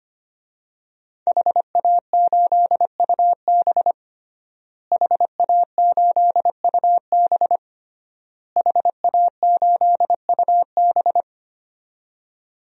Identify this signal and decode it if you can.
Morse code, heard as a keyed tone, about 25 wpm, 700 Hz. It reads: HA8UB HA8UB HA8UB
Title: DXCC-100 Call Signs - 5 Characters - Repeated 3x - 25wpm